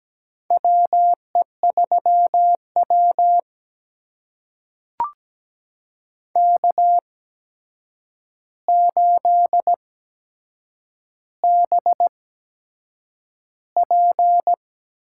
Morse code, heard as a keyed tone, about 17 wpm, spaced as for 7 wpm, 700 Hz. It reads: WE3W K8BP
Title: USA Call Signs - 4 Characters - 8x Character Spacing - 17wpm